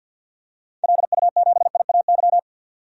Morse code, heard as a keyed tone, about 50 wpm, 700 Hz. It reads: LU7IAY
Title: DXCC-100 Call Signs - 6 Characters - 50wpm